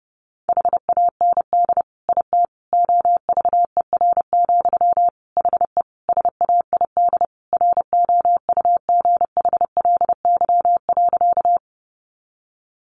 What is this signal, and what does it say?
Morse code, heard as a keyed tone, about 30 wpm, 700 Hz. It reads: HAND IT OVER, HE SAID ROUGHLY.
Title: Sentences from Top 1500 Words 30wpm